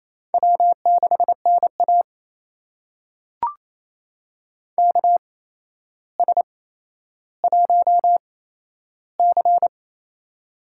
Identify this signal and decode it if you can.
Morse code, heard as a keyed tone, about 28 wpm, spaced as for 12 wpm, 700 Hz. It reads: W6NA KS1C
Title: USA Call Signs - 4 Characters - 8x Character Spacing - 28wpm